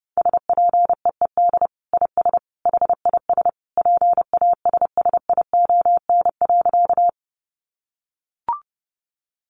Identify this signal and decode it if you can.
Morse code, heard as a keyed tone, about 29 wpm, 700 Hz. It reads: SPEED IS HIS PASSION.
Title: Sentences from Top 1300 Words 30wpm